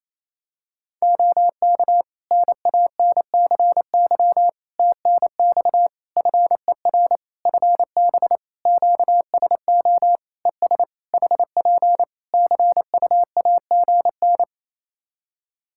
Morse code, heard as a keyed tone, about 28 wpm, 700 Hz. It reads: OK NANCY TNX FER FB QSO ES HP CUAGN
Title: Practice QSO - Exchange 4 - Ending - 28wpm